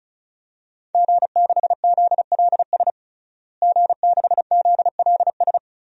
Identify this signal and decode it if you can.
Morse code, heard as a keyed tone, about 36 wpm, 690 Hz. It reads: G6ZLS G6ZLS
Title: DXCC-100 Call Signs - 5 Characters - Repeated 3x - 35wpm